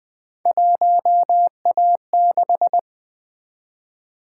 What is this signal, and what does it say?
Morse code, heard as a keyed tone, about 20 wpm, 700 Hz. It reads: 1A6
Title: Three Letter-Number Combinations - 20wpm